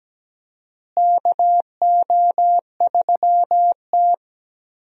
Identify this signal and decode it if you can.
Morse code, heard as a keyed tone, about 17 wpm, 700 Hz. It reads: KO3T
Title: USA Call Signs - 4 Characters - Repeated 2x - 17wpm